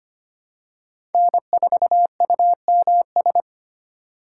Morse code, heard as a keyed tone, about 25 wpm, 700 Hz. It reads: N4UMS
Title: USA Call Signs - 5 Characters - 3x Character Spacing - 25wpm